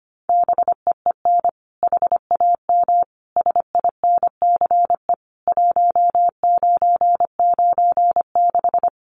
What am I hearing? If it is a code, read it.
Morse code, heard as a keyed tone, about 25 wpm, 700 Hz. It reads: BEEN HAM SINCE 1996
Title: Common QSO Phrases 25wpm